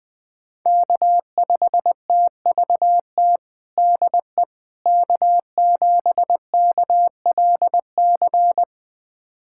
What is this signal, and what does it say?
Morse code, heard as a keyed tone, about 21 wpm, 700 Hz. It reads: K5TVT DE K7KLC
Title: USA Callsign DE Callsign - 5 Characters - 20wpm